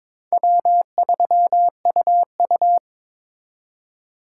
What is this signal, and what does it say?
Morse code, heard as a keyed tone, about 26 wpm, 700 Hz. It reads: W3UU
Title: Speed Racing - USA 1x2 Call Signs 22wpm